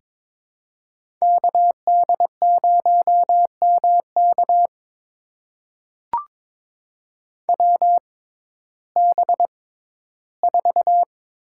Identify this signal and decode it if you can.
Morse code, heard as a keyed tone, about 22 wpm, spaced as for 13 wpm, 700 Hz. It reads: KD0MK WB4
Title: USA Call Signs - 5 Characters - 6x Character Spacing - 22wpm